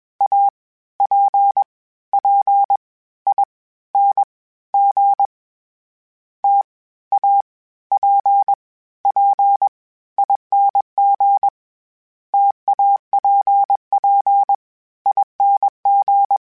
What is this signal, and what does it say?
Morse code, heard as a keyed tone, about 21 wpm, 800 Hz. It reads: A P P I N G T A P P ING TAPP ING
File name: Variable Speed / Variable Speed-mp3s / T AOEI's sp @20